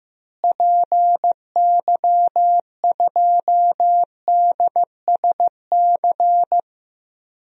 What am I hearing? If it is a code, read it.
Morse code, heard as a keyed tone, about 15 wpm, 700 Hz. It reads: PY2DSC
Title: DXCC-100 Call Signs - 6 Characters - Repeated 2x - 15wpm